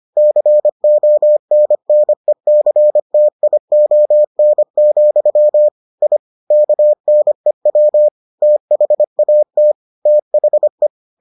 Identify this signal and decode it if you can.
Morse code, heard as a keyed tone, about 25 wpm, 600 Hz. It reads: CONNECTION, I KNEW THAT THE